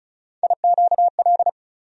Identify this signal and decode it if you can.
Morse code, heard as a keyed tone, about 34 wpm, 700 Hz. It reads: IQL
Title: USA Call Sign Postfixes - 3 Letters 35wpm